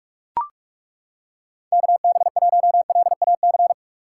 Morse code, heard as a keyed tone, about 36 wpm, 700 Hz. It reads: KB1LAC
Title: Speed Racing - USA 2x3 Call Signs 30wpm